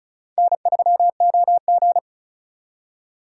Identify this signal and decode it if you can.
Morse code, heard as a keyed tone, about 45 wpm, 700 Hz. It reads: N3OG